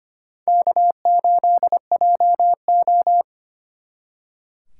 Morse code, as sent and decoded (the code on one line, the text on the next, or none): -.- ---.. .--- ---
K8JO